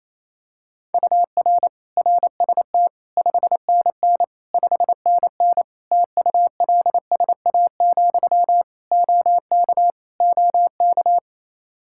Morse code, heard as a keyed tone, about 28 wpm, 700 Hz. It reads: UR RST 5NN 5NN TULSA, OK OK
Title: Practice QSO - Exchange 1 - Beginning - 28wpm